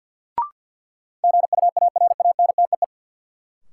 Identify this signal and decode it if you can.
Morse code, heard as a keyed tone, about 53 wpm, 700 Hz. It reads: GUARANTEE
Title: Sentences from Top 1600 Words - New Word Drill Alt - 50wpm